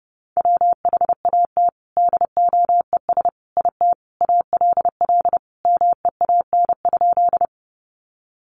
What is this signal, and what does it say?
Morse code, heard as a keyed tone, about 30 wpm, 700 Hz. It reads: WHAT DOES IT ALL MEAN?
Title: Sentences from Top 200 Words 30wpm